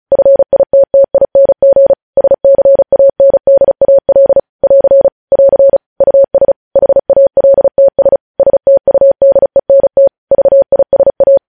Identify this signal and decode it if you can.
Morse code, heard as a keyed tone, about 36 wpm, 550 Hz. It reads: FITTING SCANDAL <AR> <AR> US HALTS STUDENT VISA
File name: podcast / News headlines-1x-35x35